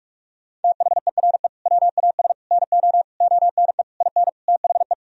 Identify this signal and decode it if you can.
Morse code, heard as a keyed tone, about 46 wpm, 700 Hz. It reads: THERE WAS NO ONE IN THE
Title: Sentences from Top 1000 Words - Review Part 3 - 45wpm